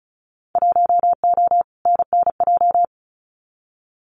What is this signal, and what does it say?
Morse code, heard as a keyed tone, about 36 wpm, 700 Hz. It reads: 1O NNJ
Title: Winter Field Day Exchanges 35wpm